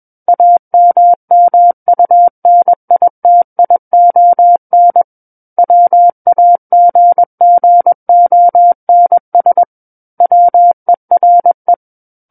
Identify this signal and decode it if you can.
Morse code, heard as a keyed tone, about 21 wpm, 700 Hz. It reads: AMMUNITION WAGGONS WERE
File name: Book One / 12-Chapter 12 - 21 WPM